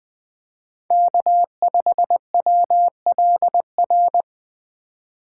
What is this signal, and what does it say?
Morse code, heard as a keyed tone, about 20 wpm, 700 Hz. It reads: K5WLR
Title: USA Call Signs - 5 Characters - 4x Character Spacing - 20wpm